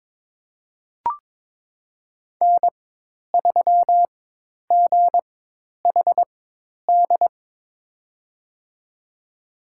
Morse code, heard as a keyed tone, about 22 wpm, spaced as for 10 wpm, 700 Hz. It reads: N3GHD
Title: USA Call Signs - 5 Characters - 4x Character Spacing - 22wpm